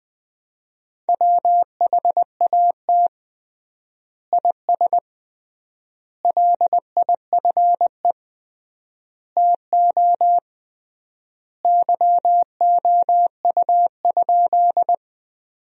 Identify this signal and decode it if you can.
Morse code, heard as a keyed tone, about 20 wpm, 700 Hz. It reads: WHAT IS LIFE TO YOU?
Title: 3x Word-Spacing - Sentences from Top 300 Words - 20wpm